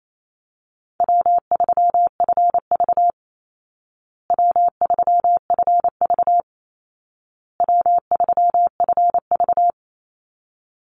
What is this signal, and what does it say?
Morse code, heard as a keyed tone, about 28 wpm, 700 Hz. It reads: W3FV W3FV W3FV
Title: Easy Call Signs Repeated 3x - 28wpm